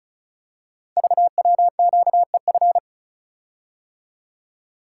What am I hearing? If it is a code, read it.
Morse code, heard as a keyed tone, about 36 wpm, 690 Hz. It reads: VWQEF
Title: Code Groups - 5 Letters - 35wpm